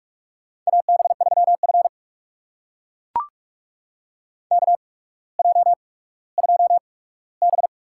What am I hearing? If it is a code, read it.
Morse code, heard as a keyed tone, about 44 wpm, spaced as for 19 wpm, 700 Hz. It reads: AB3F KJ2B
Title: USA Call Signs - 4 Characters - 8x Character Spacing - 45wpm